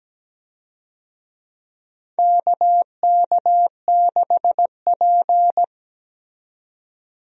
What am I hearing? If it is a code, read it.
Morse code, heard as a keyed tone, about 17 wpm, 700 Hz. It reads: KK6P
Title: USA Call Signs - 4 Characters - 17wpm